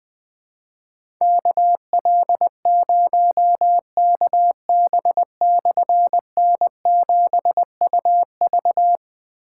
Morse code, heard as a keyed tone, about 20 wpm, 700 Hz. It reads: KL0KB/N7UV
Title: Difficult Call Signs 20wpm